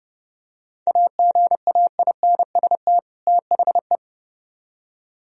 Morse code, heard as a keyed tone, about 29 wpm, 700 Hz. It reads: AGAINST THE